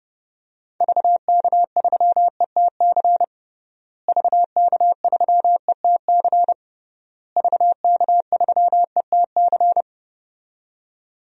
Morse code, heard as a keyed tone, about 32 wpm, 700 Hz. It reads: VK3ETC VK3ETC VK3ETC
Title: DXCC-100 Call Signs - 6 Characters - Repeated 3x - 30wpm